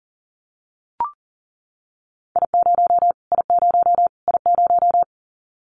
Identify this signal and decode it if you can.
Morse code, heard as a keyed tone, about 40 wpm, 700 Hz. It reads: I0 I0 I0
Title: Callsign Prefixes for Top 25 DXCC - Repeated 3x - 40wpm